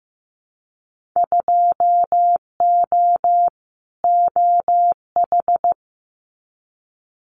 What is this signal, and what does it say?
Morse code, heard as a keyed tone, about 15 wpm, 700 Hz. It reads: 2O OH
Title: Winter Field Day Exchanges 15wpm